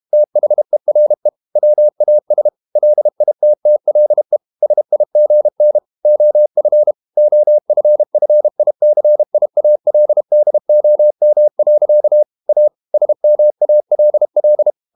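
Morse code, heard as a keyed tone, about 32 wpm, 600 Hz. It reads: THERE WAS LITTLE SIGN OF OFFICIALDOM. A SMALL
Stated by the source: Morse code practice at 32 wpm from CW med Gnister, reading Ken Follett's novel Never in English.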